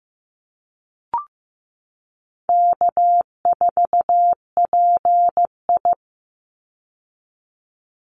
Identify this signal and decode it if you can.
Morse code, heard as a keyed tone, about 15 wpm, 700 Hz. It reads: K4PI
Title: Easy Call Signs Repeated 2x - 15wpm